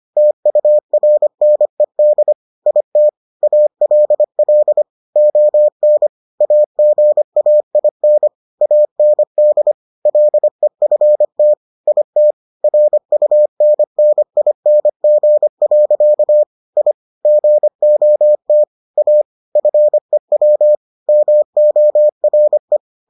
Morse code, heard as a keyed tone, about 25 wpm, 600 Hz. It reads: TURNED IT ALL ON AGAIN AND LEFT IT RUNNING. I GOT A FEW MORE